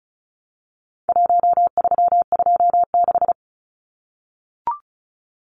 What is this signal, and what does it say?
Morse code, heard as a keyed tone, about 34 wpm, 700 Hz. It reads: 1326
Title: Numbers - Contest Style - 4-Digit 35wpm